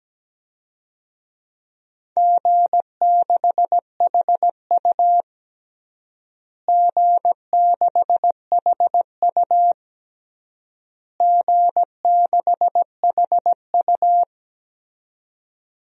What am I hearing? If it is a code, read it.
Morse code, heard as a keyed tone, about 17 wpm, 700 Hz. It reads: G6HU G6HU G6HU
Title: DXCC-100 Call Signs - 4 Characters - Repeated 3x - 17wpm